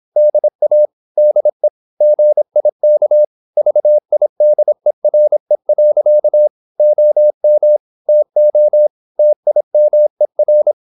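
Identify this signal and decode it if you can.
Morse code, heard as a keyed tone, about 26 wpm, 600 Hz. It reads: DA DE GIK VIDERE. OM TO TIMER
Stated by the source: High-speed (QRQ) Morse at 26 wpm.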